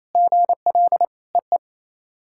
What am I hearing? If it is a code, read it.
Morse code, heard as a keyed tone, about 28 wpm, 700 Hz. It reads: GL EE
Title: SOTA QSO Practice 28wpm